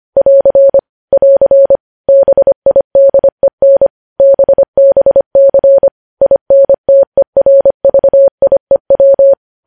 Morse code, heard as a keyed tone, about 25 wpm, 550 Hz. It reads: <AR> <AR> BIDEN BBC INTERVIEW
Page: news headlines in morse code at 25 wpm